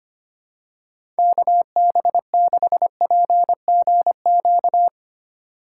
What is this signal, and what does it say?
Morse code, heard as a keyed tone, about 25 wpm, 690 Hz. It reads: KB6PGQ
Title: USA Call Signs - 6 Characters - 3x Character Spacing - 25wpm